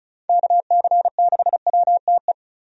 Morse code, heard as a keyed tone, about 34 wpm, 700 Hz. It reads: KC6WTE
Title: USA Call Signs - 6 Characters - Repeated 2x - 35wpm